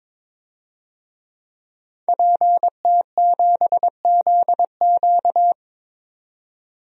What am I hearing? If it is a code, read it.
Morse code, heard as a keyed tone, about 22 wpm, 700 Hz. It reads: PT7ZQ
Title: DXCC-100 Call Signs - 5 Characters - 22wpm